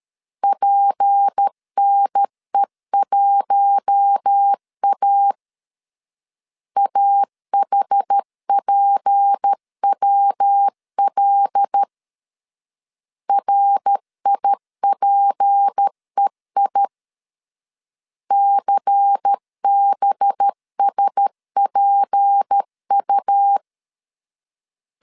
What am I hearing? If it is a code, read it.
Morse code, heard as a keyed tone, about 13 wpm, 800 Hz. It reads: PNE1A AHPWL RIPEI CBSPU
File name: LEZIONI / 6 PKX0 / 6 P 12wpm 2033